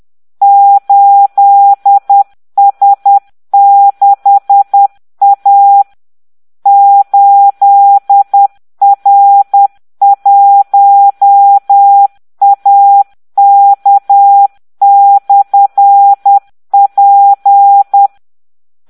Morse code, heard as a keyed tone, about 10 wpm, 800 Hz. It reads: 8S6A 8R1AK/P